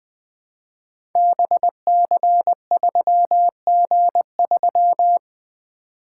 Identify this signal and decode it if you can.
Morse code, heard as a keyed tone, about 20 wpm, 700 Hz. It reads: BC3G3
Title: Code Groups - 5 Letter:Numbers - 20wpm